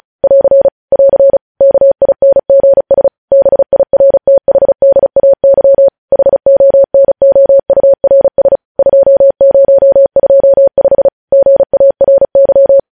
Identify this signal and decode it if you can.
Morse code, heard as a keyed tone, about 36 wpm, 550 Hz. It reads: <AR> <AR> KINGS BIRTHDAY HONOURS 2025 GARY
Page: news headlines in morse code at 35 wpm